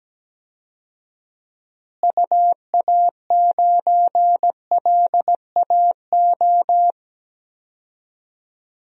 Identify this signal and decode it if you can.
Morse code, heard as a keyed tone, about 17 wpm, 700 Hz. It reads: UA9LAO
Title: DXCC-100 Call Signs - 6 Characters - Repeated 2x - 17wpm